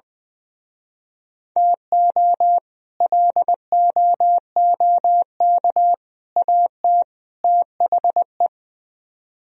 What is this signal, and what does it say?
Morse code, heard as a keyed tone, about 22 wpm, 700 Hz. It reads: TO LOOK AT THE
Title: Speed-Racing - Sets of 4 Words 20wpm